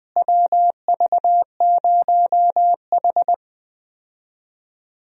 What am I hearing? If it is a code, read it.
Morse code, heard as a keyed tone, about 20 wpm, 700 Hz. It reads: WV0H
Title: USA Call Signs - 4 Characters - Repeated 3x - 20wpm